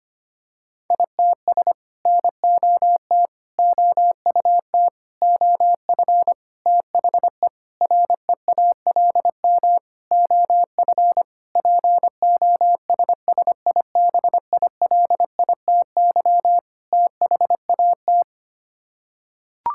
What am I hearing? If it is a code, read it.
Morse code, heard as a keyed tone, about 25 wpm, 700 Hz. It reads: ITS NOT OUT OF THE REALM OF POSSIBILITY THAT
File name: Sets of 9 Words - 25wpm